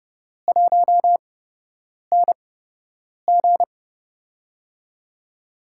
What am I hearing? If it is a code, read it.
Morse code, heard as a keyed tone, about 29 wpm, spaced as for 8 wpm, 700 Hz. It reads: 1NG
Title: USA Call Signs - 4 Characters - 8x Character Spacing - 30wpm